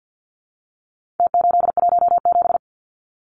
Morse code, heard as a keyed tone, about 48 wpm, 700 Hz. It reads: T817
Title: Numbers - Contest Style - 4-Digit 50wpm